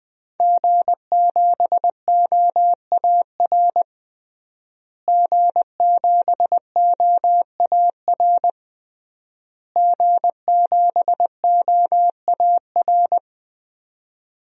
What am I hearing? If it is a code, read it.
Morse code, heard as a keyed tone, about 20 wpm, 700 Hz. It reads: G7OAR G7OAR G7OAR
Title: DXCC-100 Call Signs - 5 Characters - Repeated 3x - 20wpm